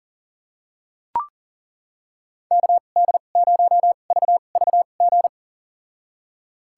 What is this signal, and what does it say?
Morse code, heard as a keyed tone, about 40 wpm, spaced as for 29 wpm, 700 Hz. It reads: KD0VVG
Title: USA Call Signs - 2x Character Spacing - 40wpm